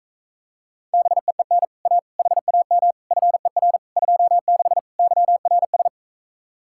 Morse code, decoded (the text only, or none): BEEN A HAM FER 26 YRS